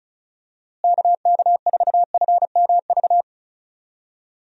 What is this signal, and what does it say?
Morse code, heard as a keyed tone, about 36 wpm, 700 Hz. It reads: KK4FMV
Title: USA Call Signs - 6 Characters - Repeated 2x - 35wpm